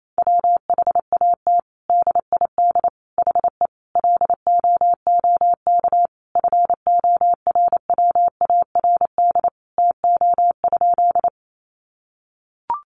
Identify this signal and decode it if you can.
Morse code, heard as a keyed tone, about 28 wpm, 700 Hz. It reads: WHAT DID HE LOOK FORWARD TO?
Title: Sentences from Top 1300 Words 28wpm